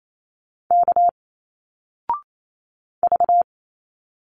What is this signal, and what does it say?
Morse code, heard as a keyed tone, about 28 wpm, 700 Hz. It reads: K V
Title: Single Letters 28wpm